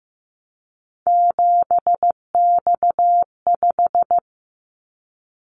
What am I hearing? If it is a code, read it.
Morse code, heard as a keyed tone, about 15 wpm, 700 Hz. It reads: 7X5
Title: Callsign Prefixes for Top 100 DXCC - 15wpm